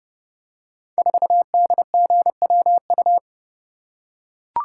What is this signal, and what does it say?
Morse code, heard as a keyed tone, about 30 wpm, 700 Hz. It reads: 4DGWU
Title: Code Groups - 5 Letter:Numbers - 30wpm